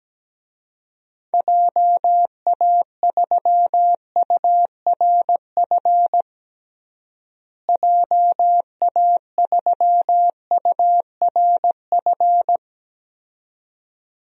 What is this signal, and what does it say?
Morse code, heard as a keyed tone, about 17 wpm, 700 Hz. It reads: JA3URF JA3URF
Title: DXCC-100 Call Signs - 6 Characters - Repeated 2x - 17wpm